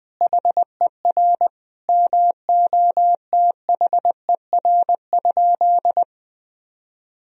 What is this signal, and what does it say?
Morse code, heard as a keyed tone, about 20 wpm, 700 Hz. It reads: HER MOTHER?
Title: Sentences from Top 1000 Words - Review Part 2 - 20wpm